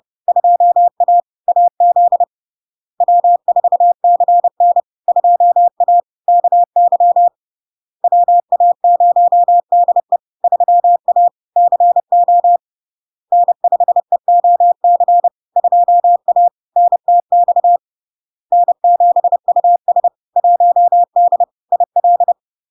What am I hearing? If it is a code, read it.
Morse code, heard as a keyed tone, about 30 wpm, 700 Hz. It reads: 2A AZ W4CN 2A KY WA0DE 3A CO N5EOC 2A NTX N7US 1D IL
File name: LARK-FD-2022-Log-in-CW-30wpm-700-Hz